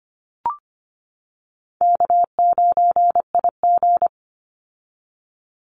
Morse code, heard as a keyed tone, about 25 wpm, 700 Hz. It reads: K9IG
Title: Call Signs - 25wpm